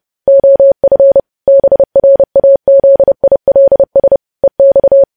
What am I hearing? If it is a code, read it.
Morse code, heard as a keyed tone, about 30 wpm, 550 Hz. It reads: OF BRAZILS EX
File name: podcast / News headlines-1x-30x30